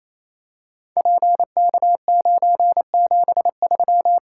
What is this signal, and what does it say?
Morse code, heard as a keyed tone, about 28 wpm, 700 Hz. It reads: PK973
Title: Code Groups - 5 Letter:Numbers - 28wpm